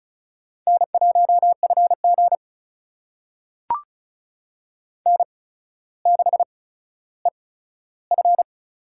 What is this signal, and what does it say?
Morse code, heard as a keyed tone, about 36 wpm, spaced as for 15 wpm, 700 Hz. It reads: N1FG N6EF
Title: USA Call Signs - 4 Characters - 8x Character Spacing - 35wpm